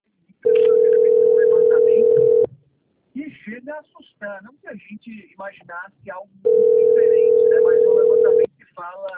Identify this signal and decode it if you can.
Ringback tone (North America), 440 + 480 Hz, with speech.
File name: out- - - - - .58447